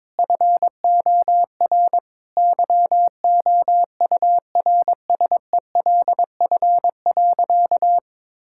Morse code, heard as a keyed tone, about 22 wpm, 700 Hz. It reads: FOR YOURSELF.
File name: practice / 22wpm / Sentences with 8-Word Phrases - 22wpm